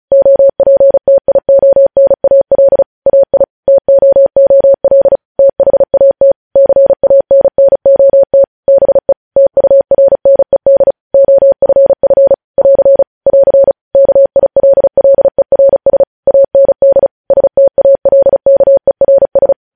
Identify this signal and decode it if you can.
Morse code, heard as a keyed tone, about 36 wpm, 550 Hz. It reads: OPTIONAL AI TOOL THAT CANNOT BE TURNED OFF <AR> <AR> KILLERS AND STALKERS
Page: news headlines in morse code at 35 wpm